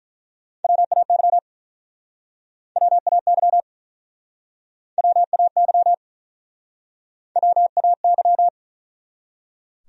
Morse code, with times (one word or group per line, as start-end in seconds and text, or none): 0.6-1.4 WAY
2.8-3.6 WAY
5.0-5.9 WAY
7.4-8.5 WAY